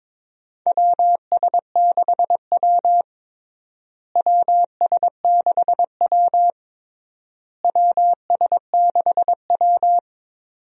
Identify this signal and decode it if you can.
Morse code, heard as a keyed tone, about 22 wpm, 700 Hz. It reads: WS6W WS6W WS6W
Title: USA Call Signs - 4 Characters - Repeated 3x - 22wpm